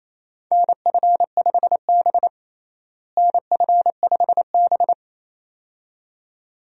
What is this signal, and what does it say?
Morse code, heard as a keyed tone, about 28 wpm, 700 Hz. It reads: NF5B NF5B
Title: USA Call Signs - 4 Characters - Repeated 2x - 28wpm